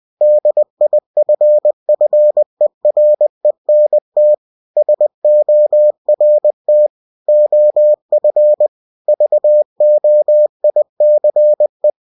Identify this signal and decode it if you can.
Morse code, heard as a keyed tone, about 20 wpm, 600 Hz. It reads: DIFFERENT SORT OF VOICE
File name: cw-books / winnie-the-pooh / winnie-20wpm-page-0014